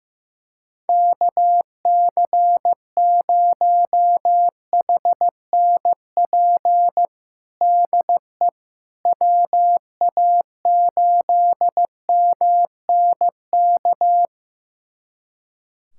Morse code, heard as a keyed tone, about 15 wpm, 700 Hz. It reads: KC0HNP DE WA8MNK
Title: USA Callsign DE Callsign 15wpm